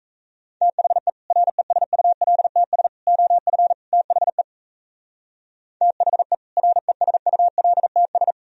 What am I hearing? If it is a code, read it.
Morse code, heard as a keyed tone, about 34 wpm, 700 Hz. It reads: THE RESULTS OF THE THE RESULTS
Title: Speed-Racing - Sets of 4 Words 28wpm